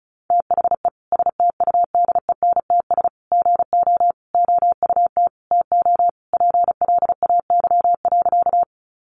Morse code, heard as a keyed tone, about 34 wpm, 700 Hz. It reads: THE STUDENTS GO OUT TO PLAY.
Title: Easy Phrases 35wpm